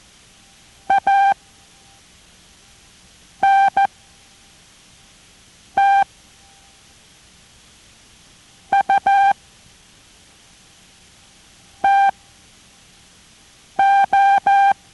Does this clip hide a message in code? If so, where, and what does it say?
Yes: Morse code from 0.9 to 14.7 s, heard as a keyed tone, about 14 wpm, spaced as for 4 wpm, 790 Hz. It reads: ANTUTO